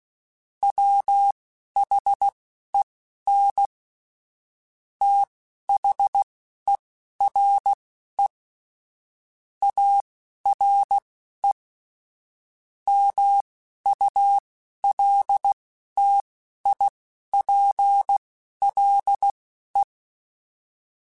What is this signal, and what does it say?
Morse code, heard as a keyed tone, about 16 wpm, spaced as for 11 wpm, 800 Hz. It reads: WHEN THERE ARE MULTIPLE
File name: Technical Subjects / Technical Subjects-mp3s / AND Gate @10